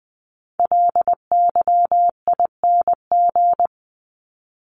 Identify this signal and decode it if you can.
Morse code, heard as a keyed tone, about 20 wpm, 700 Hz. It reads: LYING
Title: ING Suffix - Top 100 Words with Base Words 20wpm